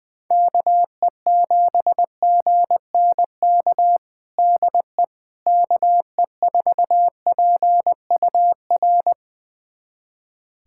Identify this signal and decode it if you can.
Morse code, heard as a keyed tone, about 20 wpm, 700 Hz. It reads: KE7GNK DE KE4PUR
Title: USA Callsign DE Callsign - 6 Characters - 20wpm